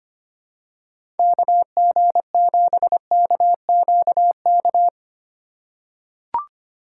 Morse code, heard as a keyed tone, about 25 wpm, 700 Hz. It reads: KG7KQK
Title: USA Call Signs - 3x Character Spacing - 25wpm